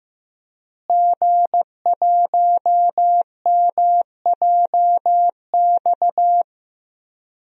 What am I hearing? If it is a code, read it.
Morse code, heard as a keyed tone, about 16 wpm, 700 Hz. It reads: G1MJX
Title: DXCC-100 Call Signs - 5 Characters - Repeated 3x - 15wpm